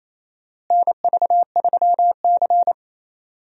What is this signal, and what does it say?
Morse code, heard as a keyed tone, about 28 wpm, 700 Hz. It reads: NV3C
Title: USA Call Signs - 4 Characters - Repeated 2x - 28wpm